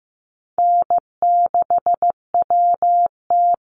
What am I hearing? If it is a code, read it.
Morse code, heard as a keyed tone, about 15 wpm, 700 Hz. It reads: N6WT
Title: NA SOTA Chaser Callsigns 15wpm